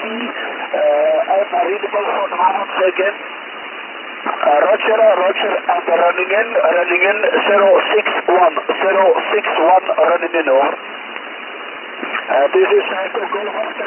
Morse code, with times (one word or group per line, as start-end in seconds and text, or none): none